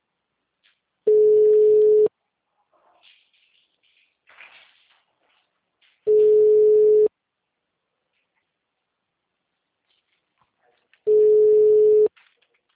Ringback tone (Europe, 425 Hz).
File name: out- -2044- - - .43551